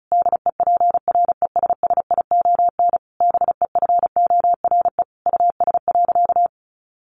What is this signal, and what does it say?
Morse code, heard as a keyed tone, about 34 wpm, 700 Hz. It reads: DEPRESSION BEFORE US.
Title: Sentences from Top 1300 Words 35wpm